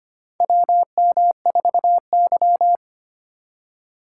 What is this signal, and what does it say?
Morse code, heard as a keyed tone, about 25 wpm, 700 Hz. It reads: WM4Y
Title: USA Call Signs - 2x Character Spacing - 25wpm